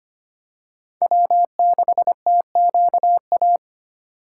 Morse code, heard as a keyed tone, about 25 wpm, 700 Hz. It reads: W6TQA
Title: USA Call Signs - 5 Characters - 6x Character Spacing - 25wpm